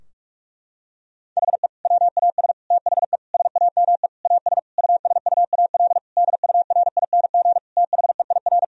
Morse code, heard as a keyed tone, about 46 wpm, 700 Hz. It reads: HE WAS THE SAME AS USUAL DURING THEIR
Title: Sentences from Top 1000 Words - Review Part 2 - 45wpm